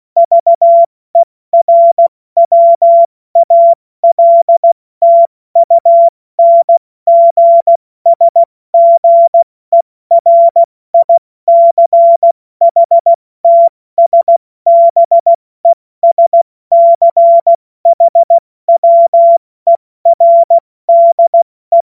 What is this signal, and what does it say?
Morse code, heard as a keyed tone, about 16 wpm, 680 Hz. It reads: VERWALTUNGSGERICHTSBESCHWERDE
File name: Morsekurs / Witze / Witz 25. Oktober 2022 70 F80